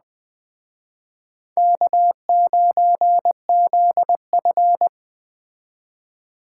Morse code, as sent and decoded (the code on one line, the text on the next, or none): -.- ----. --.. ..-.
K9ZF